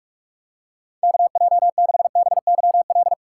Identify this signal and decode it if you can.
Morse code, heard as a keyed tone, about 44 wpm, 700 Hz. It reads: KJ6BYL